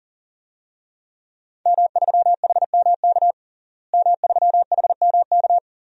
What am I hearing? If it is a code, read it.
Morse code, heard as a keyed tone, about 40 wpm, 700 Hz. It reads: M3HMK M3HMK
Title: DXCC-100 Call Signs - 5 Characters - Repeated 2x - 40wpm